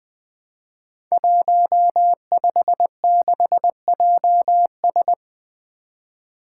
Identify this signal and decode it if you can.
Morse code, heard as a keyed tone, about 20 wpm, 700 Hz. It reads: 156JS
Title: Five Challenging Letter-Number Combinations 20wpm